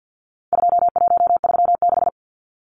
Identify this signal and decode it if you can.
Morse code, heard as a keyed tone, about 50 wpm, 700 Hz. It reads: 2136
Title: Numbers - 4-Digit 50wpm